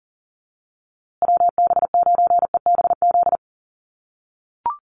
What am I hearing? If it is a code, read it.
Morse code, heard as a keyed tone, about 40 wpm, 700 Hz. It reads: WB9EBZ
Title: USA Call Signs Repeated 3x 40wpm